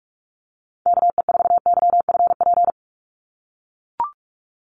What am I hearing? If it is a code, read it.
Morse code, heard as a keyed tone, about 44 wpm, 700 Hz. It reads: KE4YFP
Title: USA Call Signs 45wpm